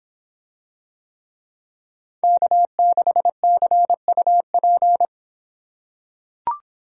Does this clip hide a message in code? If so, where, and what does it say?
Yes: Morse code from 2.2 to 5.0 s, heard as a keyed tone, about 26 wpm, 700 Hz. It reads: K6CUP